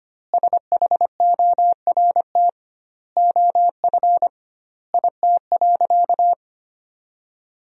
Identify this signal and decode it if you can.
Morse code, heard as a keyed tone, about 25 wpm, 700 Hz. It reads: SHORT OF IT.
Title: 2x Word-Spacing - Sentences from Top 400 Words - 25wpm